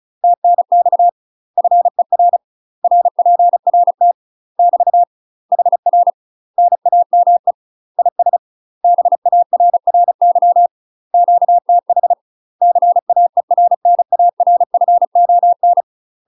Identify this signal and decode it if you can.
Morse code, heard as a keyed tone, about 34 wpm, 700 Hz. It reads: TNX FER RPRT <BT> HR NAME IS BARRY QTH CAERNARFON
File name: UK MASTER QSOs TRAINING / 35WPM / UK MASTER QSOs 0039